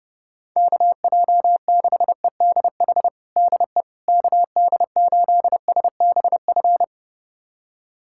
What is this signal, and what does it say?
Morse code, heard as a keyed tone, about 30 wpm, 690 Hz. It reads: KJ6EDH DE KD8SBF
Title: USA Callsign DE Callsign - 6 Characters - 30wpm